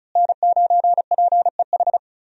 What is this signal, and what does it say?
Morse code, heard as a keyed tone, about 34 wpm, 700 Hz. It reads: N9PEH
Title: USA Call Signs - 5 Characters - Repeated 2x - 35wpm